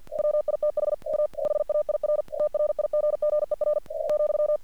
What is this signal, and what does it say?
Morse code, heard as a keyed tone, about 48 wpm, 600 Hz. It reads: WITH A HAIR TRIGGER ,
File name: archived / CRSnet